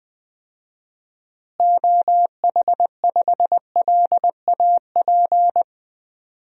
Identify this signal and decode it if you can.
Morse code, heard as a keyed tone, about 20 wpm, 700 Hz. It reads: OH5LAP
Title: DXCC-100 Call Signs - 6 Characters - Repeated 3x - 20wpm